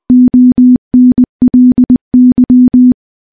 Morse code, heard as a keyed tone, about 20 wpm, 250 Hz. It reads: ONLY
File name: morse-training-20WPM